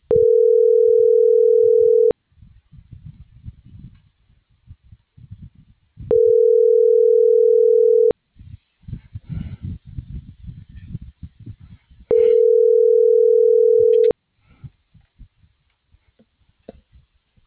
Ringback tone (North America), 440 + 480 Hz.